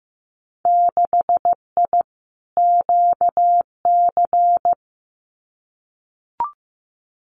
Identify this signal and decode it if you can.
Morse code, heard as a keyed tone, about 15 wpm, 700 Hz. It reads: 6I QC
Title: Winter Field Day Exchanges Encore 15wpm